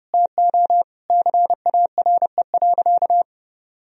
Morse code, heard as a keyed tone, about 32 wpm, 700 Hz. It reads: TO CARE.
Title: Sentences from Top 1000 Words - Review Part 4 - 30wpm